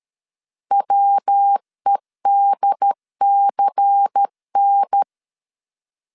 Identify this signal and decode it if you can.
Morse code, heard as a keyed tone, about 13 wpm, 800 Hz. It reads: WEDCN
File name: LEZIONI / 6 PKX0 / 6 P 12wpm 2033